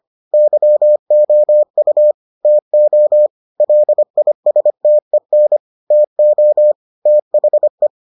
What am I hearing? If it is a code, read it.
Morse code, heard as a keyed tone, about 25 wpm, 600 Hz. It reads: YOU TO LISTEN TO THE